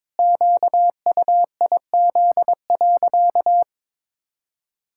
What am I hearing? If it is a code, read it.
Morse code, heard as a keyed tone, about 22 wpm, 700 Hz. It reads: QUIZ.
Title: Pangram Challenge v2 - 22wpm